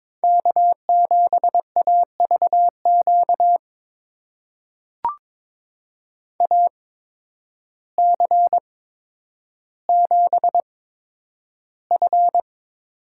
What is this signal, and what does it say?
Morse code, heard as a keyed tone, about 22 wpm, spaced as for 10 wpm, 700 Hz. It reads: K7AVQ AC7F
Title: USA Call Signs - 5 Characters - 8x Character Spacing - 22wpm